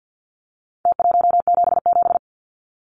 Morse code, heard as a keyed tone, about 50 wpm, 700 Hz. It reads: T177
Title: Numbers - Contest Style - 4-Digit 50wpm